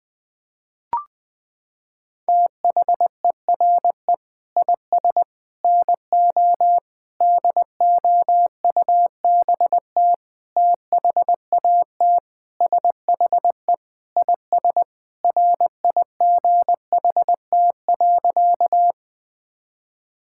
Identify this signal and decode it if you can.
Morse code, heard as a keyed tone, about 20 wpm, 700 Hz. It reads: THERE IS NO DOUBT THAT SHE IS RIGHT.
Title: Sentences with 5-Word Phrases - 20wpm